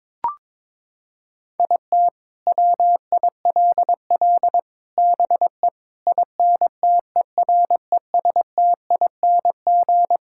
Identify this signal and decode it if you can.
Morse code, heard as a keyed tone, about 22 wpm, 700 Hz. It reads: IT WILL BE INTERESTING
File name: practice / 22wpm / Sets of 8 Words - 22wpm